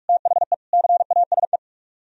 Morse code, heard as a keyed tone, about 54 wpm, 700 Hz. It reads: THE CASE
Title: Speed-Racing - Sets of 3 Words 45wpm